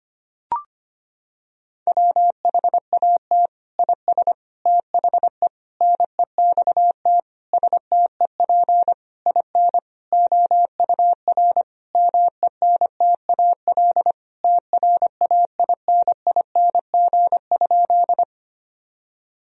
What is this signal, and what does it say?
Morse code, heard as a keyed tone, about 25 wpm, 700 Hz. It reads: WHAT IS THE NEXT STEP IN OUR MENTAL TRAINING?
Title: Sentences from Top 1000 Words - Review Part 4 - 25wpm